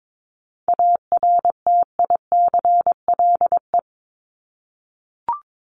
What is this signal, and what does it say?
Morse code, heard as a keyed tone, about 22 wpm, 700 Hz. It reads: ARTICLE
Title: Top 751-1000 Words 22wpm